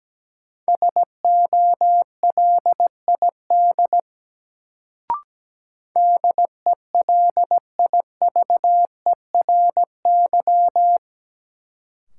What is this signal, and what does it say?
Morse code, heard as a keyed tone, about 17 wpm, 700 Hz. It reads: SOLID DELIVERY
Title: Sentences from Top 1500 Words - New Word Drill Alt - 17wpm